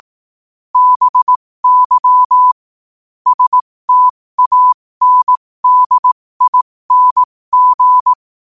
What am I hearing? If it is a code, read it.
Morse code, heard as a keyed tone, about 18 wpm, spaced as for 15 wpm, 1000 Hz. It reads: BY STANDING
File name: qotdfiles / 15wpm